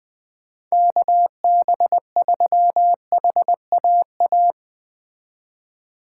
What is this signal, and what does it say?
Morse code, heard as a keyed tone, about 20 wpm, 700 Hz. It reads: KB3HAA
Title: Speed Racing - USA 2x3 Call Signs 17wpm